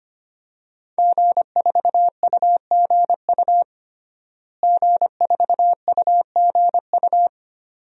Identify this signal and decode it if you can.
Morse code, heard as a keyed tone, about 25 wpm, 700 Hz. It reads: G4UGU G4UGU
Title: DXCC-100 Call Signs - 5 Characters - Repeated 2x - 25wpm